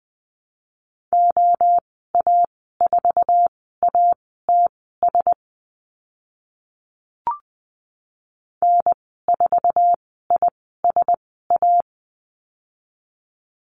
Morse code, heard as a keyed tone, about 20 wpm, spaced as for 14 wpm, 700 Hz. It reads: OA4ATS N4ISA
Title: 055 Lesson 10g - Letter r - Call Signs 20wpm